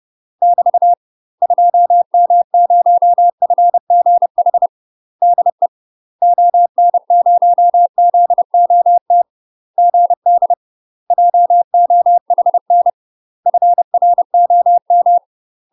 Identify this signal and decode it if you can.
Morse code, heard as a keyed tone, about 30 wpm, 700 Hz. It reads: <BT> 2M0FGH DE ON0ZOT GD JOHN FROM
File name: UK MASTER QSOs TRAINING / 30WPM / UK MASTER QSOs 0028